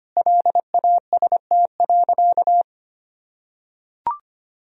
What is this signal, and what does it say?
Morse code, heard as a keyed tone, about 25 wpm, 700 Hz. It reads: LAST.
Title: Sentences from Top 300 Words and Abbreviations 25wpm